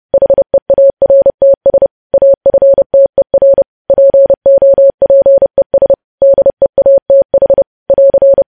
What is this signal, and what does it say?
Morse code, heard as a keyed tone, about 30 wpm, 550 Hz. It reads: HEARTS AFTER POPES DEATH <AR>
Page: News Headlines in Morse Code at 30 WPM